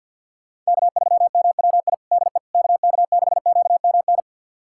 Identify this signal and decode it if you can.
Morse code, heard as a keyed tone, about 50 wpm, 700 Hz. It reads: K3MWI DE KK6QMN
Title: USA Callsign DE Callsign 50wpm